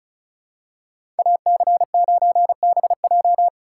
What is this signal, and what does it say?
Morse code, heard as a keyed tone, about 36 wpm, 700 Hz. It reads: AC9BJ
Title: USA Call Signs - 5 Characters - 4x Character Spacing - 35wpm